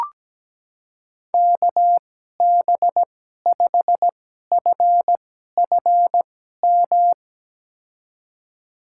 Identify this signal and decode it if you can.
Morse code, heard as a keyed tone, about 17 wpm, spaced as for 12 wpm, 700 Hz. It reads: KB5FFM
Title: USA Call Signs - 6 Characters - 2x Character Spacing - 17wpm